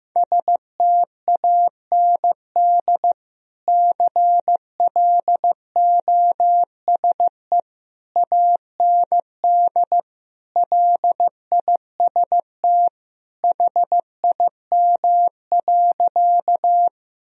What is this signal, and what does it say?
Morse code, heard as a keyed tone, about 15 wpm, 700 Hz. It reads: STAND CLOSE AND LIST HIM.
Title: Sentences from Top 1000 Words - Review Part 2 - 15wpm